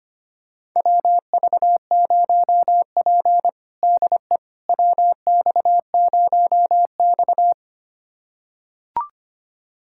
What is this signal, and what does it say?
Morse code, heard as a keyed tone, about 25 wpm, 700 Hz. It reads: WV0P DE WX0X
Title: USA Callsign DE Callsign - 4 Characters - 25wpm